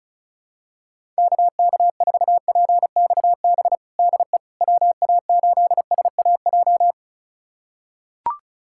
Morse code, heard as a keyed tone, about 34 wpm, 700 Hz. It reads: KK4PXB DE WA8SAJ
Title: USA Callsign DE Callsign - 6 Characters - 35wpm